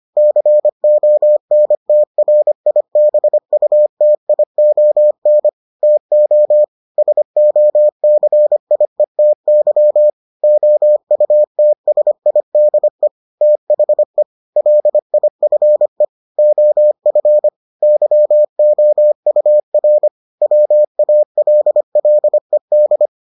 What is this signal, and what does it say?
Morse code, heard as a keyed tone, about 25 wpm, 600 Hz. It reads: CONTRIBUTION TO SOCIETY OUTSIDE THE LIFE OF YOUR WALLED